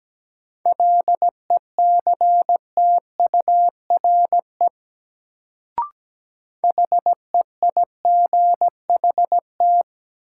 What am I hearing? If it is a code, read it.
Morse code, heard as a keyed tone, about 17 wpm, 700 Hz. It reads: LECTURE HEIGHT
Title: Sentences from Top 1600 Words - New Word Drill - 17wpm